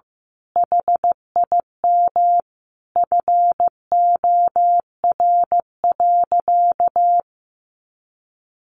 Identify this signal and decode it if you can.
Morse code, heard as a keyed tone, about 15 wpm, 700 Hz. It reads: HIM FOR.
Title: Sentences from Top 1000 Words 15wpm